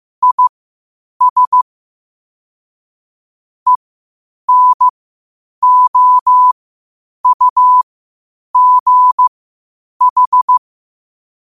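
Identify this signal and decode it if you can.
Morse code, heard as a keyed tone, about 15 wpm, spaced as for 8 wpm, 1000 Hz. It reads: IS ENOUGH
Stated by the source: Morse practice at 8 WPM from Morse Resource.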